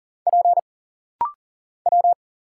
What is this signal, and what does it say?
Morse code, heard as a keyed tone, about 40 wpm, 700 Hz. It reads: P W
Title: Single Letters - Rapid-Fire 40wpm